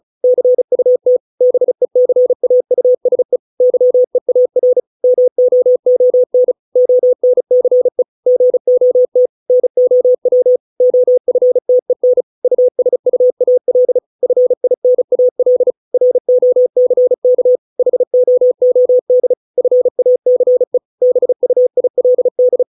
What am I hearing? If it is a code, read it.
Morse code, heard as a keyed tone, about 34 wpm, 500 Hz. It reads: CUT BECAUSE YEAR MOON ONCE GOT NOW OFTEN USUAL FINAL ROCK HOOD FACE BUILD